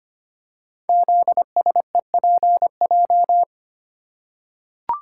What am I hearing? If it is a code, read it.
Morse code, heard as a keyed tone, about 25 wpm, 700 Hz. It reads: ZSEPJ
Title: Code Groups - 5 Letters - 25wpm